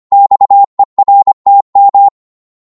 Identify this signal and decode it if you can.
Morse code, heard as a keyed tone, about 25 wpm, 800 Hz. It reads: XERTM